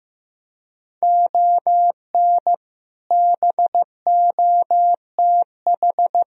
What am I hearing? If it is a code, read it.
Morse code, heard as a keyed tone, about 15 wpm, 700 Hz. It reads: ON BOTH
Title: Speed-Racing - Sets of 5 Words 15wpm